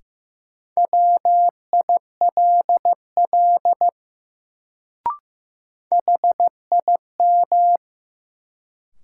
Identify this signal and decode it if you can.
Morse code, heard as a keyed tone, about 15 wpm, 700 Hz. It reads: WILL HIM
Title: Sentences from Top 100 Words - New Word Drill - 15wpm